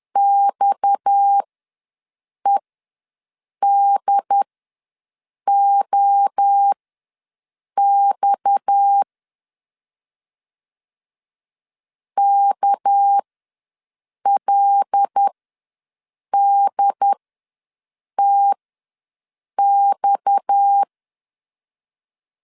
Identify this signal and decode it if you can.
Morse code, heard as a keyed tone, about 10 wpm, spaced as for 6 wpm, 800 Hz. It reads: XEDOX KLDTX
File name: LEZIONI / 6 PKX0 / 6 PKX ws 10wpm 3706